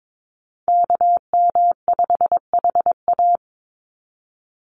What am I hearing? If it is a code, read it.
Morse code, heard as a keyed tone, about 22 wpm, 700 Hz. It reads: KM5HA
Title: USA Call Signs 22wpm